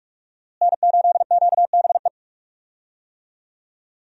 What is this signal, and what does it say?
Morse code, heard as a keyed tone, about 41 wpm, 700 Hz. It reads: N8QBE
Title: USA Call Signs - 5 Characters - Repeated 2x - 45wpm